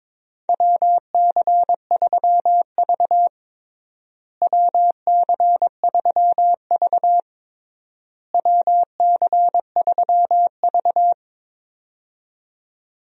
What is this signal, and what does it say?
Morse code, heard as a keyed tone, about 22 wpm, 700 Hz. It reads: WC3V WC3V WC3V
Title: USA Call Signs - 4 Characters - Repeated 3x - 22wpm